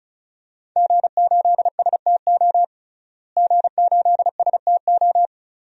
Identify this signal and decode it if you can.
Morse code, heard as a keyed tone, about 36 wpm, 700 Hz. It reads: G8STO G8STO
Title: DXCC-100 Call Signs - 5 Characters - Repeated 3x - 35wpm